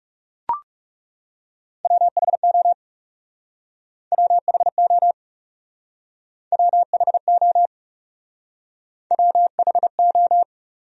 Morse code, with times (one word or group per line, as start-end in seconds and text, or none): 1.9-2.7 WHO
4.1-5.1 WHO
6.5-7.7 WHO
9.1-10.4 WHO